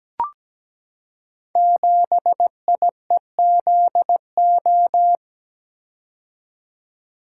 Morse code, heard as a keyed tone, about 17 wpm, 700 Hz. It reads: 7IEZO
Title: Code Groups - 5 Letter:Numbers - 17wpm